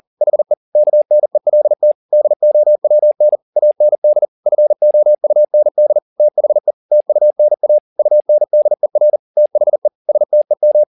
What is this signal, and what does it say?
Morse code, heard as a keyed tone, about 40 wpm, 600 Hz. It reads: HE KNELT DOWN AND FOUND THE TUNA UNDER THE STEM